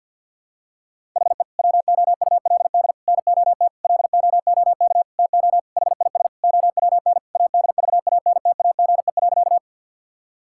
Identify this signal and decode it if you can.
Morse code, heard as a keyed tone, about 49 wpm, 700 Hz. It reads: HE WOULD NOT LOOK TO HIS OWN ADVANTAGE.
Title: Sentences from Top 1000 Words - Review Part 6 - 50wpm